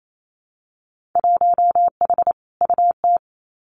Morse code, heard as a keyed tone, about 28 wpm, 700 Hz. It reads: 1H UT
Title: Winter Field Day Exchanges Encore 28wpm